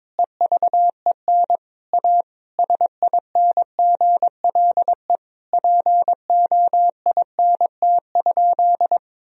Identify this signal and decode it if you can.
Morse code, heard as a keyed tone, about 22 wpm, 700 Hz. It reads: EVEN A SINGLE POINT?
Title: Sentences from Top 1000 Words - Review Part 4 - 22wpm